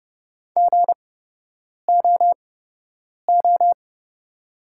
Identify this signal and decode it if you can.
Morse code, heard as a keyed tone, about 30 wpm, spaced as for 8 wpm, 700 Hz. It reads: GOO
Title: 8x Character Spacing - Top 300 Words - 30wpm